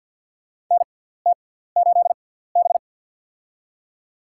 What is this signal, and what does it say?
Morse code, heard as a keyed tone, about 53 wpm, spaced as for 17 wpm, 700 Hz. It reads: NT8B
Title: USA Call Signs - 4 Characters - 6x Character Spacing - 50wpm